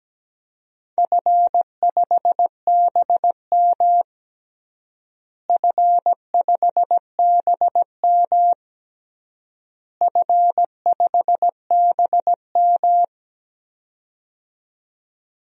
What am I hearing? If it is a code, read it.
Morse code, heard as a keyed tone, about 17 wpm, 700 Hz. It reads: F5BM F5BM F5BM
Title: DXCC-100 Call Signs - 4 Characters - Repeated 3x - 17wpm